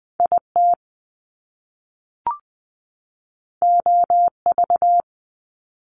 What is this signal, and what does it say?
Morse code, heard as a keyed tone, about 20 wpm, 700 Hz. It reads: IT OV
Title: Top Two Letter Combinations in Words 20wpm